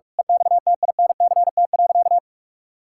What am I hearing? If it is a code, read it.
Morse code, heard as a keyed tone, about 46 wpm, 700 Hz. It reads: EXTINCT.
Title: Pangram Challenge v2 - 45wpm